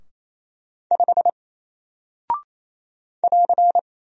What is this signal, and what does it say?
Morse code, heard as a keyed tone, about 27 wpm, 700 Hz. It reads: 5 <AR>
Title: Single Character (Letter:Number:Punctuation:Prosign) 28wpm